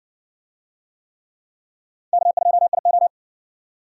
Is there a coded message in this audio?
Yes: Morse code from 2.1 to 3.1 s, heard as a keyed tone, about 60 wpm, 700 Hz. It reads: K2IO